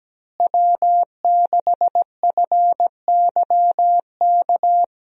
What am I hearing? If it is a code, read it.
Morse code, heard as a keyed tone, about 17 wpm, 700 Hz. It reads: W6FYK
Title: USA Call Signs - 5 Characters - Repeated 2x - 17wpm